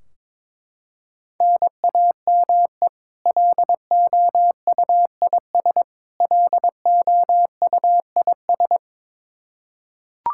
Morse code, heard as a keyed tone, about 22 wpm, 700 Hz. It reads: NAME LOUIS LOUIS
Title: K1USN SST QSO Practice 22wpm